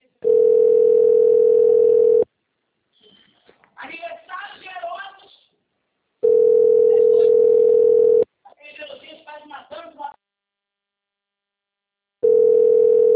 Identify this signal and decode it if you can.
Ringback tone (North America), 440 + 480 Hz, with speech.
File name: rg-1600-unknown- - - .63412